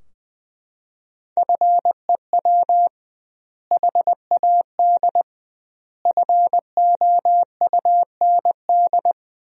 Morse code, heard as a keyed tone, about 20 wpm, 700 Hz. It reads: FEW HAD FOUND
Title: 2x Word-Spacing - Sentences from Top 300 Words - 20wpm